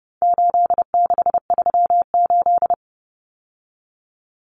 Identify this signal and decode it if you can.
Morse code, heard as a keyed tone, about 30 wpm, 700 Hz. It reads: 8638
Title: Common Cut Numbers - 4-Digit 30wpm